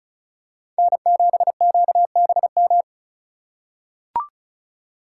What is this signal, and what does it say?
Morse code, heard as a keyed tone, about 34 wpm, 690 Hz. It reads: N7QBM
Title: USA Call Signs - 5 Characters - 2x Character Spacing - 35wpm